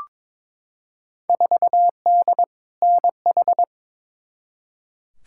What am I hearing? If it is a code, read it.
Morse code, heard as a keyed tone, about 22 wpm, 700 Hz. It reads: 4D NH
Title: Field Day Exchanges Encore - 22wpm